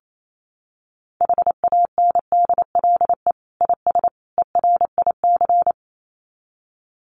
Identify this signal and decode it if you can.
Morse code, heard as a keyed tone, about 28 wpm, 700 Hz. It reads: HANDLE IS ERIC
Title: Common QSO Phrases 28wpm